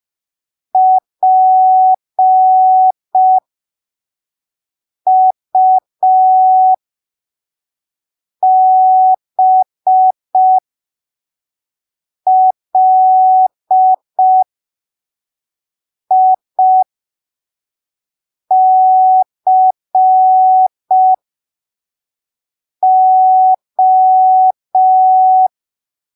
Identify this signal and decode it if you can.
Morse code, heard as a keyed tone, about 5 wpm, spaced as for 3 wpm, 750 Hz. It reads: PUBLICO
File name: arquivos / texto / texto 01 5ppm